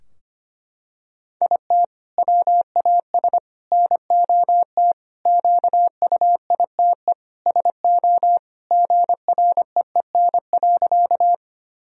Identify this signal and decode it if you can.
Morse code, heard as a keyed tone, about 25 wpm, 700 Hz. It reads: IT WAS NOT QUITE SO GREEN.
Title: Sentences from Top 1000 Words - Review Part 2 - 25wpm